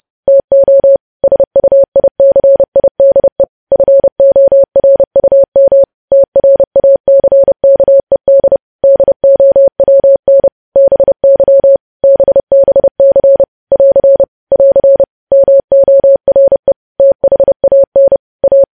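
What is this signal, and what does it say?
Morse code, heard as a keyed tone, about 30 wpm, 550 Hz. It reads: TO SUICIDE FORUM TRACKED DOWN BY BBC <AR> <AR> MORE THAN A
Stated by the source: Morse news headlines at 30 wpm.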